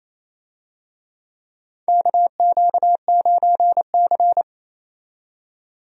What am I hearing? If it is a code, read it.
Morse code, heard as a keyed tone, about 28 wpm, 700 Hz. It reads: KQ9C
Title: USA Call Signs - 4 Characters - 28wpm